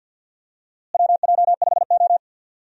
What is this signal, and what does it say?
Morse code, heard as a keyed tone, about 46 wpm, 700 Hz. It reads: WJ5O
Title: USA Call Signs - 4 Characters - Repeated 3x - 50wpm